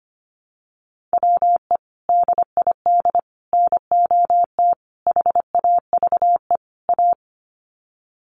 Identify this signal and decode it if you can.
Morse code, heard as a keyed tone, about 25 wpm, 700 Hz. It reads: WE DID NOT HAVE A
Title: Sets of 5 Words Encore2 25wpm